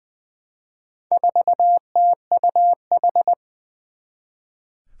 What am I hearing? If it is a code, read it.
Morse code, heard as a keyed tone, about 20 wpm, 700 Hz. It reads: 4TUH
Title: Four Challenging Letter-Number Combinations 20wpm